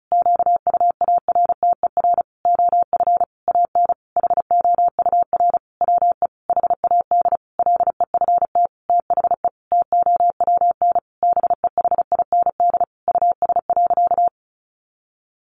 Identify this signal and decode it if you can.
Morse code, heard as a keyed tone, about 34 wpm, 700 Hz. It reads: QUARTER OF AN HOUR WE HAD LEFT THE TOWN BEHIND US.
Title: Sentences from Top 1300 Words 35wpm